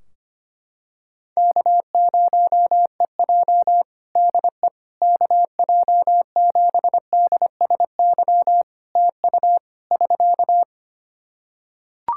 Morse code, heard as a keyed tone, about 25 wpm, 700 Hz. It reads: K0EJ DE KJ7DSY TU <SK>
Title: Practice QSO - Exchange 4 - Ending - 25wpm